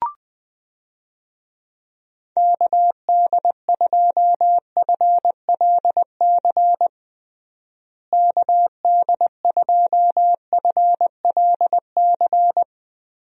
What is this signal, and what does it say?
Morse code, heard as a keyed tone, about 20 wpm, 700 Hz. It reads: KD2FLC KD2FLC
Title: USA Call Signs - 6 Characters - Repeated 2x - 20wpm